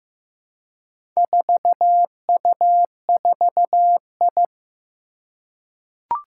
Morse code, heard as a keyed tone, about 15 wpm, 700 Hz. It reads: 4U4I
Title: Four Challenging Letter-Number Combinations 15wpm